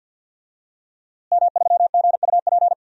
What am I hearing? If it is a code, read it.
Morse code, heard as a keyed tone, about 52 wpm, 700 Hz. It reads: M3GUP
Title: DXCC-100 Call Signs - 5 Characters - Repeated 2x - 50wpm